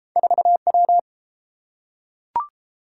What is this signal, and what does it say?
Morse code, heard as a keyed tone, about 31 wpm, 700 Hz. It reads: 4W